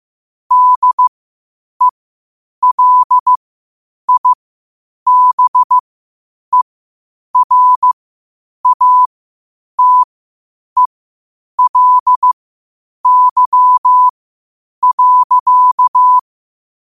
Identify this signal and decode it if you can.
Morse code, heard as a keyed tone, about 15 wpm, spaced as for 8 wpm, 1000 Hz. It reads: DELIBERATELY.
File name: qotdfiles / 8wpm